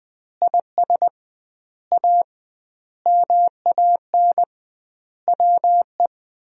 Morse code, heard as a keyed tone, about 21 wpm, 700 Hz. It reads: IS A MAN WE
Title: 2x Word-Spacing - Sentences from Top 200 Words - 20wpm